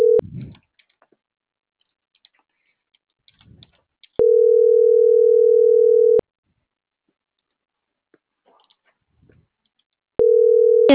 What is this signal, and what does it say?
Ringback tone (North America), 440 + 480 Hz.